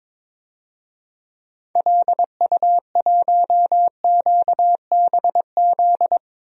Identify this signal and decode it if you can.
Morse code, heard as a keyed tone, about 22 wpm, 700 Hz. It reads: LU1QBZ
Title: DXCC-100 Call Signs - 6 Characters - 22wpm